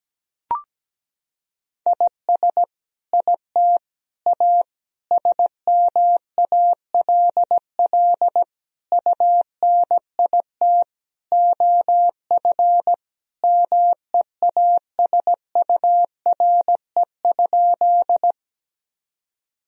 Morse code, heard as a keyed tone, about 17 wpm, 700 Hz. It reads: IS IT A SMALL UNIT OF MEASURE?
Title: Sentences from Top 1000 Words - Review Part 2 - 17wpm